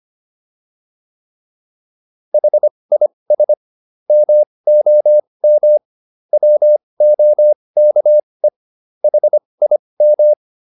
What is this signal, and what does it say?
Morse code, heard as a keyed tone, about 25 wpm, spaced as for 20 wpm, 600 Hz. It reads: HIS MOM WOKE HIM